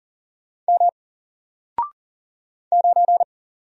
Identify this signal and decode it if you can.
Morse code, heard as a keyed tone, about 40 wpm, 700 Hz. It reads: M 9
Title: Single Character (Letter:Number:Punctuation:Prosign) Alt 40wpm